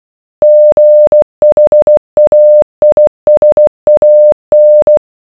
Morse code, heard as a keyed tone, about 16 wpm, 600 Hz. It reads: GHAISAN